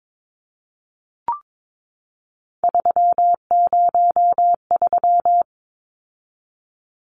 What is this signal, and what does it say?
Morse code, heard as a keyed tone, about 22 wpm, 700 Hz. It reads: 303
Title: Numbers - 3-Digit 22wpm